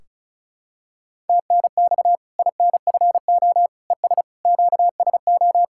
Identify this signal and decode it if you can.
Morse code, heard as a keyed tone, about 36 wpm, 700 Hz. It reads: TNX INFO ES QSO
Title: Practice QSO - Exchange 4 - Ending - 35wpm